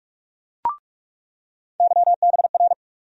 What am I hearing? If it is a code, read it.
Morse code, heard as a keyed tone, about 43 wpm, 700 Hz. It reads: YBR
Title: USA Call Sign Postfixes - 3 Letters 45wpm